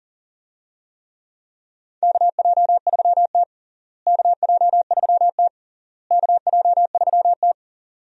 Morse code, heard as a keyed tone, about 40 wpm, 700 Hz. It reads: KJ3T KJ3T KJ3T
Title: USA Call Signs - 4 Characters - Repeated 3x - 40wpm